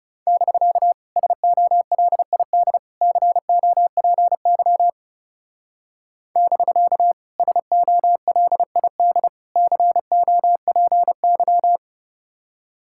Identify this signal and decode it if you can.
Morse code, heard as a keyed tone, about 39 wpm, 700 Hz. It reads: <BK> SOLID COPY <BK> SOLID COPY